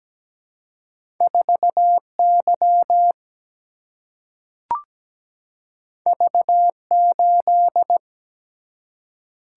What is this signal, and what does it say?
Morse code, heard as a keyed tone, about 17 wpm, 700 Hz. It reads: 4 Y V 8
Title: Single-Letter Single-Number 17wpm